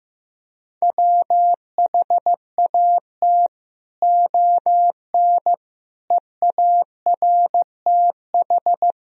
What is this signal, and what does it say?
Morse code, heard as a keyed tone, about 15 wpm, 700 Hz. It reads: WHAT ON EARTH
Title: Sentences from Top 200 Words and Abbreviations 15wpm